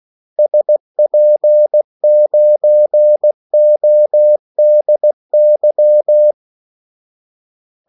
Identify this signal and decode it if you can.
Morse code, heard as a keyed tone, about 16 wpm, 600 Hz. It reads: SP9ODY
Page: Kaldesignaler 16 wpm